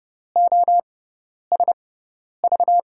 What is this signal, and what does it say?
Morse code, heard as a keyed tone, about 30 wpm, spaced as for 10 wpm, 700 Hz. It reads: OSV
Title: USA Call Signs - 5 Characters - 6x Character Spacing - 30wpm